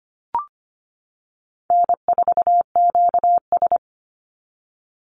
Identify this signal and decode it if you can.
Morse code, heard as a keyed tone, about 25 wpm, 700 Hz. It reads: N4QS
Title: Easy Call Signs Repeated 2x - 25wpm